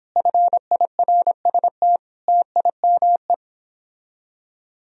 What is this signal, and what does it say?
Morse code, heard as a keyed tone, about 31 wpm, 700 Hz. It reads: FIRST TIME